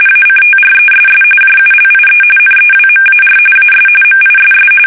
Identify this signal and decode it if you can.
RTTY, about 45.45 baud. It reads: ACTRESS JULIE NEWMAR AS MISS
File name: fsk850 / battleshipnj